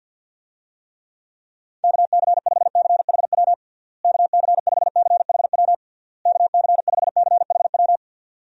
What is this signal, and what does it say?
Morse code, heard as a keyed tone, about 52 wpm, 700 Hz. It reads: KC5CHW KC5CHW KC5CHW
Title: USA Call Signs - 6 Characters - Repeated 3x - 50wpm